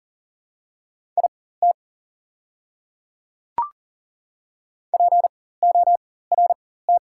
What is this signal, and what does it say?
Morse code, heard as a keyed tone, about 40 wpm, spaced as for 19 wpm, 690 Hz. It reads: IT PORT
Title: 4x Character Spacing - Top 500 Words - 40wpm